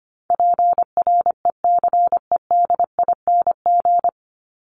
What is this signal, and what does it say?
Morse code, heard as a keyed tone, about 25 wpm, 700 Hz. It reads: PRECEDING
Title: ING Suffix - Top 100 Words with Base Words 25wpm